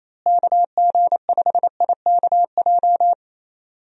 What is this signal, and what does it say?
Morse code, heard as a keyed tone, about 28 wpm, 700 Hz. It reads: KG5IKJ
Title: USA Call Signs - 6 Characters - Repeated 2x - 28wpm